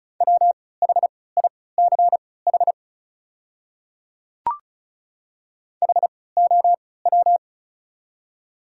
Morse code, heard as a keyed tone, about 34 wpm, spaced as for 20 wpm, 690 Hz. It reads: WHICH HOW
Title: 3x Character Spacing - Top 200 Words - 35wpm